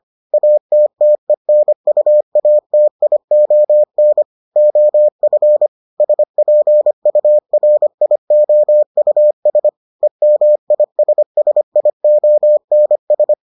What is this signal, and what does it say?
Morse code, heard as a keyed tone, about 25 wpm, 600 Hz. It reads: ATTENUATION OF SPURIOUS EMISSIONS